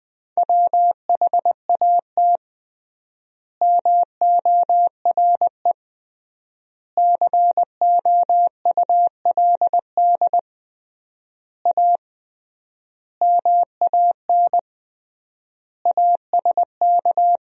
Morse code, heard as a keyed tone, about 21 wpm, 700 Hz. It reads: WHAT MORE COULD A MAN ASK
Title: 3x Word-Spacing - Sentences from Top 200 Words - 20wpm